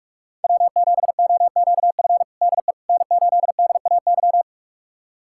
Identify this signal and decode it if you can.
Morse code, heard as a keyed tone, about 42 wpm, 700 Hz. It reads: W7OQF DE N8DAY
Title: USA Callsign DE Callsign - 5 Characters - 45wpm